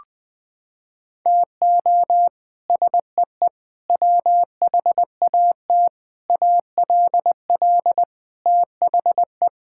Morse code, heard as a keyed tone, about 21 wpm, 700 Hz. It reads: TO SEE WHAT ALL THE
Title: Sets of 8 Words - 20wpm